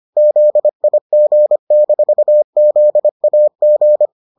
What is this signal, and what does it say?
Morse code, heard as a keyed tone, about 25 wpm, 600 Hz. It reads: ZIG-ZAG